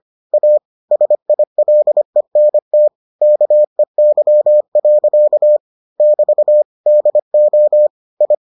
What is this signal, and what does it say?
Morse code, heard as a keyed tone, about 25 wpm, 600 Hz. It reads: A SILENT KEY. <BT> DO I